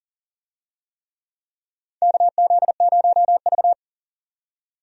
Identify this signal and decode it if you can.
Morse code, heard as a keyed tone, about 40 wpm, 700 Hz. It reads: KZ0V
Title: USA Call Signs - 4 Characters - 40wpm